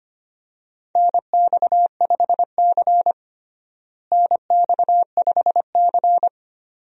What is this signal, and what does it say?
Morse code, heard as a keyed tone, about 25 wpm, 700 Hz. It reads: NX5C NX5C
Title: USA Call Signs - 4 Characters - Repeated 2x - 25wpm